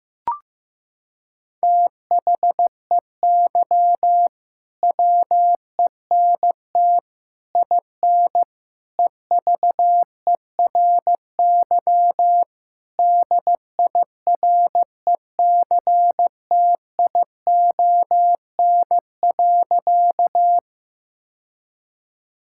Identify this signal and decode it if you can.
Morse code, heard as a keyed tone, about 16 wpm, 700 Hz. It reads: THEY WENT IN EVERY DIRECTION.
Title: Sentences from Top 1000 Words - Review Part 5 - 15wpm